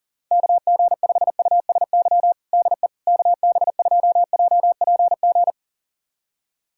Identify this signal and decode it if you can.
Morse code, heard as a keyed tone, about 40 wpm, 700 Hz. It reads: KG5USY DE KB2JPG
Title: USA Callsign DE Callsign - 6 Characters - 40wpm